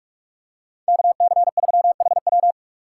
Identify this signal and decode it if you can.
Morse code, heard as a keyed tone, about 45 wpm, 700 Hz. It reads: KC3HW